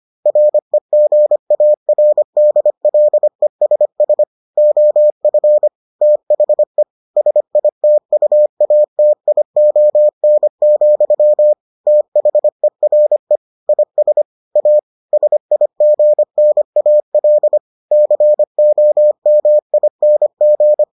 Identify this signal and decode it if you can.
Morse code, heard as a keyed tone, about 25 wpm, 600 Hz. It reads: REGARDLESS OF THE SITUATION, THERE IS A SIGNAL COMING